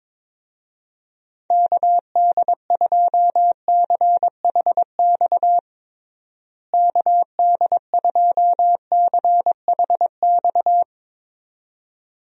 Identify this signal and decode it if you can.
Morse code, heard as a keyed tone, about 21 wpm, 700 Hz. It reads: KD2CHX KD2CHX
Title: USA Call Signs - 6 Characters - Repeated 2x - 22wpm